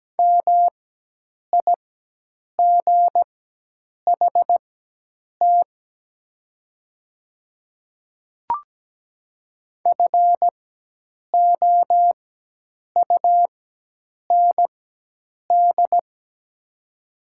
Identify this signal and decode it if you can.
Morse code, heard as a keyed tone, about 17 wpm, spaced as for 8 wpm, 700 Hz. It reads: MIGHT FOUND
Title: 4x Character Spacing - Top 300 Words - 17wpm